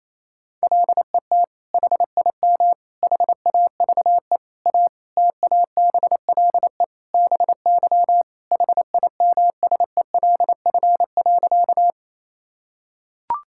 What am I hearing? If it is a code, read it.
Morse code, heard as a keyed tone, about 28 wpm, 700 Hz. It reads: LET HIM HAVE A TABLE BY HIMSELF.
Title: Sentences from Top 1600 Words 28wpm